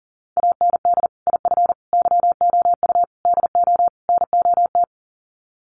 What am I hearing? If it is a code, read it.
Morse code, heard as a keyed tone, about 40 wpm, 700 Hz. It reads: AND IF YOU DO NOT
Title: Sets of 5 Words 40wpm